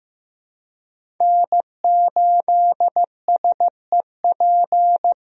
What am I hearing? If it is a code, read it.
Morse code, heard as a keyed tone, about 15 wpm, 700 Hz. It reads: N8SEP
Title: USA Call Signs - 5 Characters - Repeated 3x - 15wpm